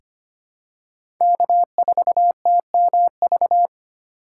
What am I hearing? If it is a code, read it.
Morse code, heard as a keyed tone, about 25 wpm, 700 Hz. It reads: K4TMV
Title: USA Call Signs - 5 Characters - 2x Character Spacing - 25wpm